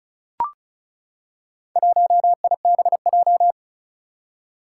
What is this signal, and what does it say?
Morse code, heard as a keyed tone, about 36 wpm, 700 Hz. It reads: 1IBJ
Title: Four Challenging Letter-Number Combinations 35wpm